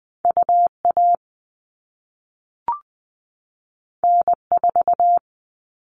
Morse code, heard as a keyed tone, about 20 wpm, 700 Hz. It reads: UA N4
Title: All Cut Numbers - 2-Digit 20wpm